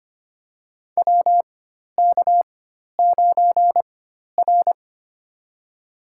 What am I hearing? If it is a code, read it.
Morse code, heard as a keyed tone, about 25 wpm, spaced as for 12 wpm, 690 Hz. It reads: WK9R
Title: USA Call Signs - 4 Characters - 4x Character Spacing - 25wpm